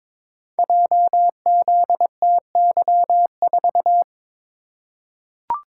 Morse code, heard as a keyed tone, about 23 wpm, 700 Hz. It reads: JZTY4
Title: Code Groups - 5 Letter:Numbers - 22wpm